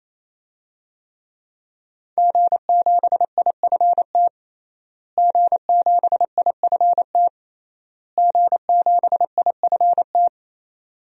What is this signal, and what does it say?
Morse code, heard as a keyed tone, about 28 wpm, 700 Hz. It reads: G7IFT G7IFT G7IFT
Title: DXCC-100 Call Signs - 5 Characters - Repeated 3x - 28wpm